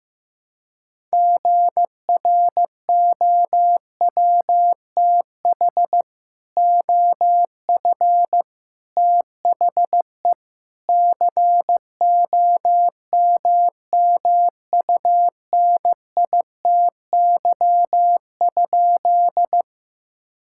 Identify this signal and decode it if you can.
Morse code, heard as a keyed tone, about 15 wpm, 700 Hz. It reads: GROWTH OF THE COMMUNITY?
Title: Sentences from Top 1000 Words - Review Part 5 - 15wpm